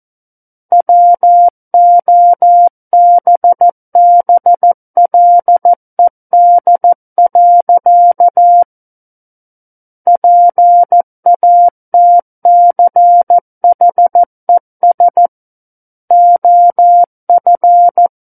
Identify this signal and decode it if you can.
Morse code, heard as a keyed tone, about 14 wpm, 700 Hz. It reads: WOBBLED. PATCHES OF
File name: Book One / 05-Chapter 5 - 14 WPM